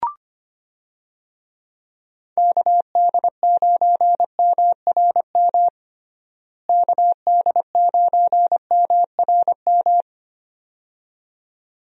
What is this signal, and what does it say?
Morse code, heard as a keyed tone, about 25 wpm, 700 Hz. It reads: KD9MRM KD9MRM
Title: USA Call Signs - 6 Characters - Repeated 2x - 25wpm